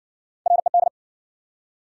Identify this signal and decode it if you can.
Morse code, heard as a keyed tone, about 56 wpm, 700 Hz. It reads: RED